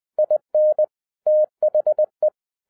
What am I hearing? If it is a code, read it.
Morse code, heard as a keyed tone, about 20 wpm, 610 Hz. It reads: IN THE
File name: 2011-07-mp3 / n08 A LOT HAS CHANGED 20wpm